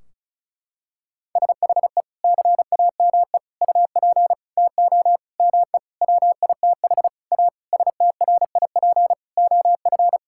Morse code, heard as a keyed tone, about 36 wpm, 700 Hz. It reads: SHE CAME UP TO ME WITH A STRIP OF
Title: Sentences from Top 1600 Words 35wpm